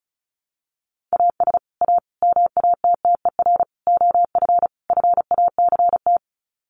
Morse code, heard as a keyed tone, about 34 wpm, 700 Hz. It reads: AS A MATTER OF FACT
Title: Sets of 5 Words 35wpm